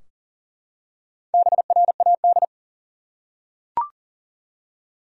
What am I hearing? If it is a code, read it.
Morse code, heard as a keyed tone, about 40 wpm, 700 Hz. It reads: BRAD
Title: USA Common Names - Max 5 Letters - 40wpm